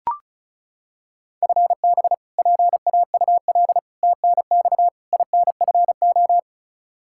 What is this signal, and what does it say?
Morse code, heard as a keyed tone, about 34 wpm, 700 Hz. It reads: FB PAUL TNX INFO
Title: Practice QSO - Exchange 3 - More Info - 35wpm